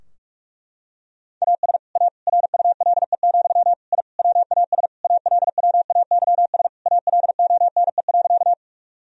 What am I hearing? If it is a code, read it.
Morse code, heard as a keyed tone, about 41 wpm, 700 Hz. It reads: AS A RULE, I WAS ALWAYS ALONE.
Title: Sentences from Top 1000 Words - Review Part 2 - 45wpm